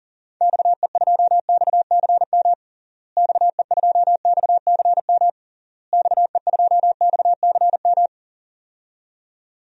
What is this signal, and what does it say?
Morse code, heard as a keyed tone, about 40 wpm, 700 Hz. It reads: XE2XCM XE2XCM XE2XCM
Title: DXCC-100 Call Signs - 6 Characters - Repeated 3x - 40wpm